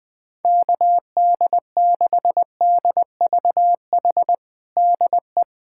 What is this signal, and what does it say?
Morse code, heard as a keyed tone, about 20 wpm, 700 Hz. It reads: KD6DVH DE
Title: USA Callsign DE Callsign 20wpm